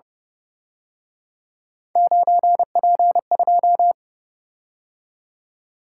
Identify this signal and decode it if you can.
Morse code, heard as a keyed tone, about 31 wpm, 700 Hz. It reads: 9P2